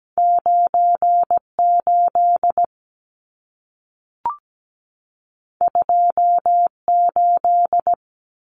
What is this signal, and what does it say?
Morse code, heard as a keyed tone, about 17 wpm, 700 Hz. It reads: 98 28
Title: Numbers - 2-Digit 17wpm